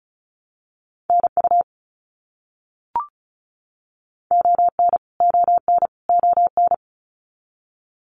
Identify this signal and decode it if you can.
Morse code, heard as a keyed tone, about 34 wpm, 700 Hz. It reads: NU ON ON ON
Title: Canadian Provinces and Territories - Repeated 3x - 35wpm